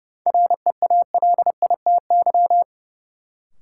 Morse code, heard as a keyed tone, about 31 wpm, 700 Hz. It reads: REALITY
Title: Sentences from Top 1200 Words - New Word Drill Alt - 30wpm